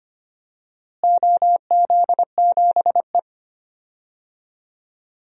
Morse code, heard as a keyed tone, about 25 wpm, 700 Hz. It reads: OZ7E
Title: DXCC-100 Call Signs - 4 Characters - Repeated 3x - 25wpm